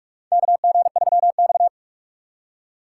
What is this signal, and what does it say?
Morse code, heard as a keyed tone, about 47 wpm, 690 Hz. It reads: KG3X
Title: USA Call Signs - 4 Characters - 8x Character Spacing - 45wpm